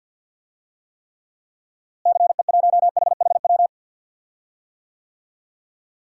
Morse code, heard as a keyed tone, about 51 wpm, 700 Hz. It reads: CE1HHW
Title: DXCC-100 Call Signs - 6 Characters - Repeated 3x - 50wpm